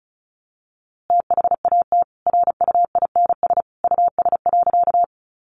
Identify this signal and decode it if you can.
Morse code, heard as a keyed tone, about 34 wpm, 700 Hz. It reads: THAT RUINS US.
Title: A Sound Of Thunder 35wpm